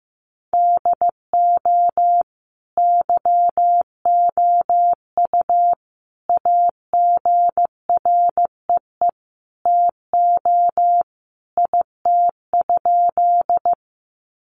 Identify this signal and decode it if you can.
Morse code, heard as a keyed tone, about 15 wpm, 700 Hz. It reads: DO YOU AGREE TO IT?
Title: Sentences from Top 1100 Words 15wpm